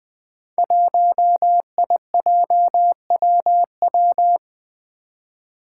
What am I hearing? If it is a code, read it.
Morse code, heard as a keyed tone, about 20 wpm, 700 Hz. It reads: 1IJWW
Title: Five Challenging Letter-Number Combinations 20wpm